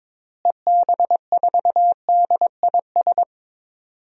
Morse code, heard as a keyed tone, about 22 wpm, 700 Hz. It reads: EB4DIS
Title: DXCC-100 Call Signs - 6 Characters - Repeated 3x - 22wpm